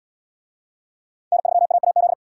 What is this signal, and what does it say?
Morse code, heard as a keyed tone, about 77 wpm, 700 Hz. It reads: N3AAZ